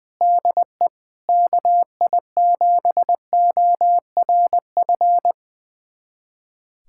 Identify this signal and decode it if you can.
Morse code, heard as a keyed tone, about 20 wpm, 700 Hz. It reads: DE KI7ORF
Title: USA Callsign DE Callsign 20wpm